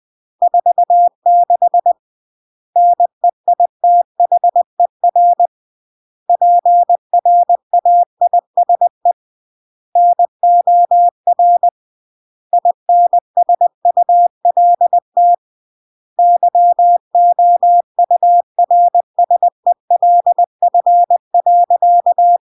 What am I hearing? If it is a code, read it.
Morse code, heard as a keyed tone, about 20 wpm, 700 Hz. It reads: 46 NEITHER PRAISE NOR INSULT YOURSELF.
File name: CW / jodorowsky / 82 commandments 20-20